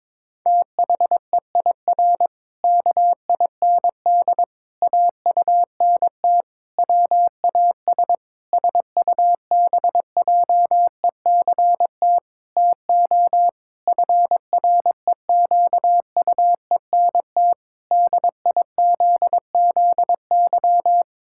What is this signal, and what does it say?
Morse code, heard as a keyed tone, about 21 wpm, 700 Hz. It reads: THEIR KIND AUNT WAS SUBJECT TO FREQUENT DIZZY
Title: Pangram Challenge v2 - 22wpm